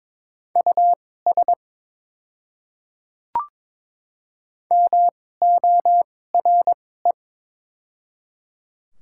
Morse code, heard as a keyed tone, about 22 wpm, spaced as for 16 wpm, 700 Hz. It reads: US MORE
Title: 2x Character Spacing - Top 200 Words - 22wpm